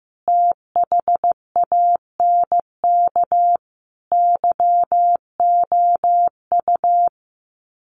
Morse code, heard as a keyed tone, about 15 wpm, 700 Hz. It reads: THANK YOU
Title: Sets of 5 Words Encore 15wpm